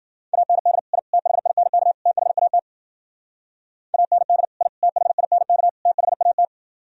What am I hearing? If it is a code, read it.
Morse code, heard as a keyed tone, about 52 wpm, 700 Hz. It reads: AND I THINK THAT AND I THINK THAT